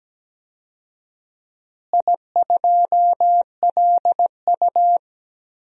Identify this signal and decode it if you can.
Morse code, heard as a keyed tone, about 17 wpm, 700 Hz. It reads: I2LU
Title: DXCC-100 Call Signs - 4 Characters - Repeated 2x - 17wpm